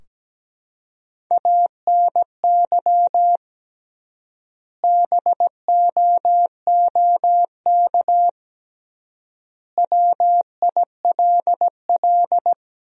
Morse code, heard as a keyed tone, about 17 wpm, 700 Hz. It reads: ANY BOOK WILL
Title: 3x Word-Spacing - Sentences from Top 300 Words - 17wpm